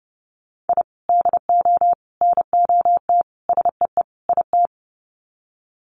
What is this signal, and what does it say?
Morse code, heard as a keyed tone, about 30 wpm, 700 Hz. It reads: I DO NOT SEE IT
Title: Sets of 5 Words Encore2 30wpm